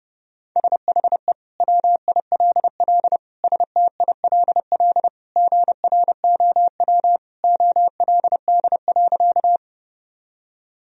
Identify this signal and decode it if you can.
Morse code, heard as a keyed tone, about 32 wpm, 700 Hz. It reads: SHE WILL STILL GROW OLD.
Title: Sentences from Top 1000 Words - Review Part 1 - 30wpm